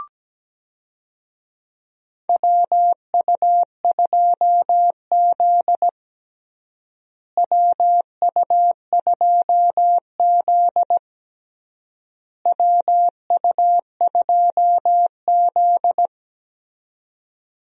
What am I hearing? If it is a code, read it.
Morse code, heard as a keyed tone, about 17 wpm, 700 Hz. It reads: WU2Z WU2Z WU2Z
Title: USA Call Signs - 4 Characters - Repeated 3x - 17wpm